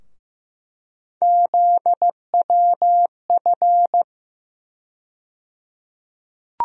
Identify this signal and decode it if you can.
Morse code, heard as a keyed tone, about 15 wpm, 700 Hz. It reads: ZWF
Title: USA Call Sign Postfixes - 3 Letters 15wpm